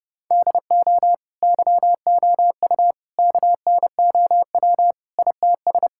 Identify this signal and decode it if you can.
Morse code, heard as a keyed tone, about 32 wpm, 700 Hz. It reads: DO YOU KNOW ITS
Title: Sentences from Top 1000 Words - Review Part 5 - 30wpm